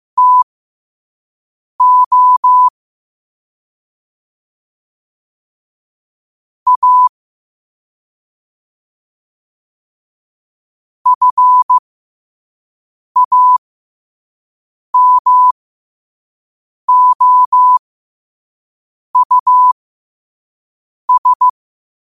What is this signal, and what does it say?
Morse code, heard as a keyed tone, about 15 wpm, spaced as for 5 wpm, 1000 Hz. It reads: TO A FAMOUS
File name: qotdfiles / 5wpm